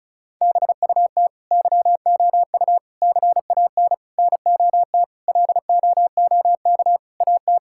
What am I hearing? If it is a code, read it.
Morse code, heard as a keyed tone, about 36 wpm, 690 Hz. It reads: BUT YOU CAN NOT LOOK AT
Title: Sentences from Top 1000 Words - Review Part 6 - 35wpm